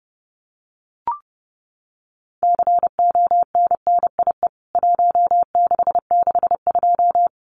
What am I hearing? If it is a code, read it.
Morse code, heard as a keyed tone, about 30 wpm, 700 Hz. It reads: CONNIE 1662
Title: CWT Exchange 30wpm